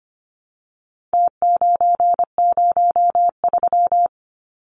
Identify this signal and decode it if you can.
Morse code, heard as a keyed tone, about 25 wpm, 700 Hz. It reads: T903
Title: Numbers - Contest Style - 4-Digit 25wpm